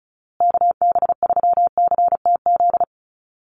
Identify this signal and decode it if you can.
Morse code, heard as a keyed tone, about 34 wpm, 700 Hz. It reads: KB3CTZ
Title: USA Call Signs Repeated 3x 35wpm